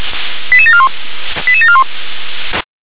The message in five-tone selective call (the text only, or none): 88741 88741